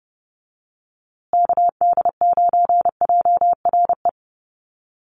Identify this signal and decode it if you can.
Morse code, heard as a keyed tone, about 30 wpm, 700 Hz. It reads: KD9JRE
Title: USA Call Signs Repeated 2x 30wpm